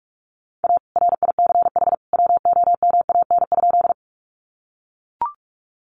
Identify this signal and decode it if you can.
Morse code, heard as a keyed tone, about 44 wpm, 700 Hz. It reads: A RICH WOMAN?
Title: Sentences from Top 1100 Words 45wpm